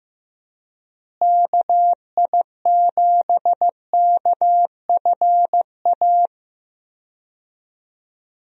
Morse code, heard as a keyed tone, about 15 wpm, 700 Hz. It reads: KI7KFA
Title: USA Call Signs - 6 Characters - Repeated 3x - 15wpm